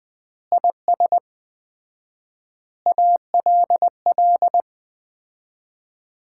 Morse code, heard as a keyed tone, about 20 wpm, 700 Hz. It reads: IS ALL
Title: 4x Word-Spacing - Sets of 5 Words - 20wpm